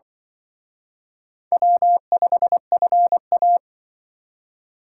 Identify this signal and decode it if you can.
Morse code, heard as a keyed tone, about 25 wpm, 700 Hz. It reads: W5FA
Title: Speed Racing - USA 1x2 Call Signs 20wpm